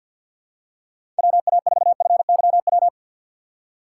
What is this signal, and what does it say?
Morse code, heard as a keyed tone, about 52 wpm, 700 Hz. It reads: WA4FYW
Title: USA Call Signs - 6 Characters - 6x Character Spacing - 50wpm